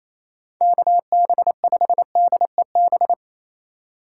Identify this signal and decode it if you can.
Morse code, heard as a keyed tone, about 35 wpm, 700 Hz. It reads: KB5DEB